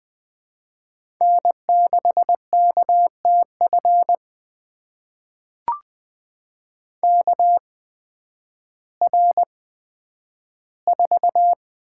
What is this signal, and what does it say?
Morse code, heard as a keyed tone, about 20 wpm, spaced as for 11 wpm, 700 Hz. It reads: N6KTF KR4
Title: USA Call Signs - 8x Character Spacing - 20wpm